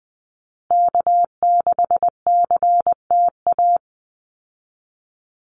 Morse code, heard as a keyed tone, about 20 wpm, 700 Hz. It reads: K6CTA
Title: Call Signs Repeated 2x - 20wpm